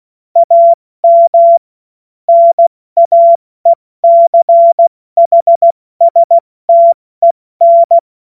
Morse code, heard as a keyed tone, about 16 wpm, 680 Hz. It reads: AM NAECHSTEN
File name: Morsekurs / Witze / Witz 12 März 2024 70 F80